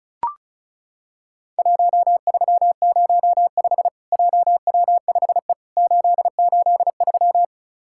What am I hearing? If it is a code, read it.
Morse code, heard as a keyed tone, about 36 wpm, 700 Hz. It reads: 1305 JW5E 883
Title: Worked All Europe DX Contest - QTC Exchange - 35wpm